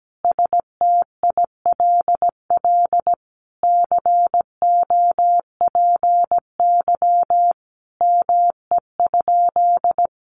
Morse code, heard as a keyed tone, about 17 wpm, 700 Hz. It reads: STILL COPY ME?
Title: Sentences from Top 1400 Words 17wpm